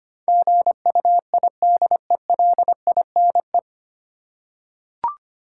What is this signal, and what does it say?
Morse code, heard as a keyed tone, about 25 wpm, 700 Hz. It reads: GUIDELINE
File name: Nine Letter Words - 25wpm